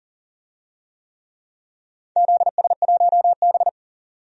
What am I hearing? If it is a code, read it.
Morse code, heard as a keyed tone, about 40 wpm, 700 Hz. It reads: ZS1B
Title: DXCC-100 Call Signs - 4 Characters - 40wpm